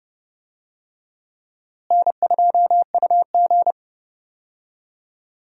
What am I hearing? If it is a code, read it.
Morse code, heard as a keyed tone, about 30 wpm, 700 Hz. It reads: N2UG
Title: USA Call Signs - 4 Characters - Repeated 2x - 30wpm